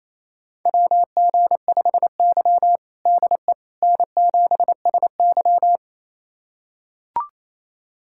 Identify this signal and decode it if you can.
Morse code, heard as a keyed tone, about 28 wpm, 700 Hz. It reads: WG5Y DE N7SY
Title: USA Callsign DE Callsign - 4 Characters - 28wpm